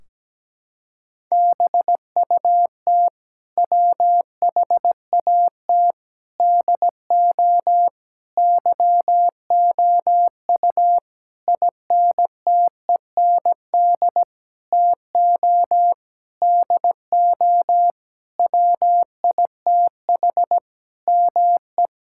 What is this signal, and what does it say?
Morse code, heard as a keyed tone, about 17 wpm, 700 Hz. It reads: BUT WHAT DO YOU INTEND TO DO WITH ME
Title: Sentences from Top 1600 Words 17wpm